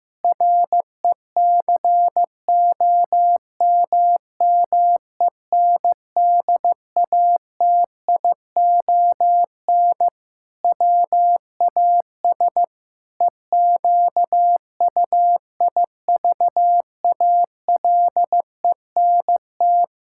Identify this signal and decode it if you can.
Morse code, heard as a keyed tone, about 15 wpm, 700 Hz. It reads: RECOMMENDATION WAS EQUIVALENT
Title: Sentences from Top 1600 Words 15wpm